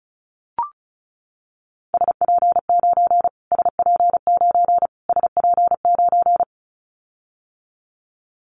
Morse code, heard as a keyed tone, about 34 wpm, 700 Hz. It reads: SP9 SP9 SP9
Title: Callsign Prefixes for Top 25 DXCC - Repeated 3x - 35wpm